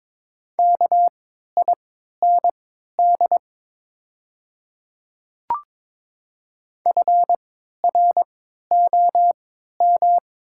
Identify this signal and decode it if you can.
Morse code, heard as a keyed tone, about 22 wpm, spaced as for 12 wpm, 690 Hz. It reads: KIND FROM
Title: 3x Character Spacing - Top 200 Words - 22wpm